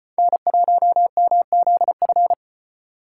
Morse code, heard as a keyed tone, about 33 wpm, 700 Hz. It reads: N1MZF